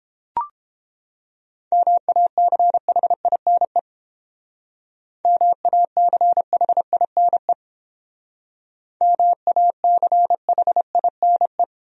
Morse code, heard as a keyed tone, about 30 wpm, 700 Hz. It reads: MACHINE MACHINE MACHINE